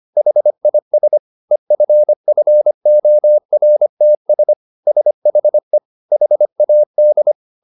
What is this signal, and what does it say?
Morse code, heard as a keyed tone, about 25 wpm, 600 Hz. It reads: HIS EFFORTS SHE HAD